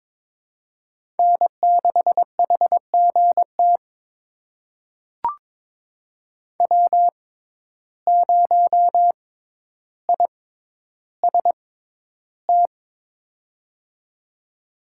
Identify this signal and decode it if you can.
Morse code, heard as a keyed tone, about 22 wpm, spaced as for 11 wpm, 700 Hz. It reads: N6HGT W0IST
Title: USA Call Signs - 5 Characters - 6x Character Spacing - 22wpm